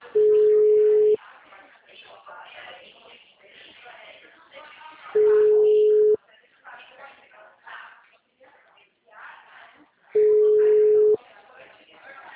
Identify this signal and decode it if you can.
Ringback tone (Europe, 425 Hz), with speech.
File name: out- -2010- - - .5002